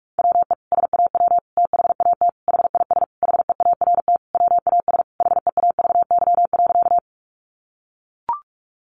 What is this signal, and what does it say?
Morse code, heard as a keyed tone, about 44 wpm, 700 Hz. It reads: WE SAW THAT HIS HEART WAS HEAVY.
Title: Sentences from Top 1200 Words 45wpm